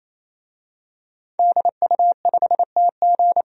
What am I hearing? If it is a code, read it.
Morse code, heard as a keyed tone, about 28 wpm, 700 Hz. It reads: DU5TG
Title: DXCC-100 Call Signs - 5 Characters - 28wpm